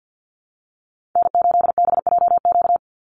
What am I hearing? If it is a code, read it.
Morse code, heard as a keyed tone, about 50 wpm, 700 Hz. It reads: N8BJQ
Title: Call Signs - 50wpm